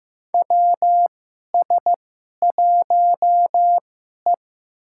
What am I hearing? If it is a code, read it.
Morse code, heard as a keyed tone, about 15 wpm, spaced as for 11 wpm, 700 Hz. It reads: WS1E
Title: USA Call Signs - 4 Characters - 2x Character Spacing - 15wpm